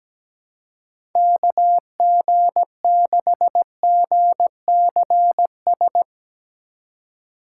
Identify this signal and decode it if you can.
Morse code, heard as a keyed tone, about 17 wpm, 700 Hz. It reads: KG6GCS
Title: USA Call Signs - 6 Characters - 2x Character Spacing - 17wpm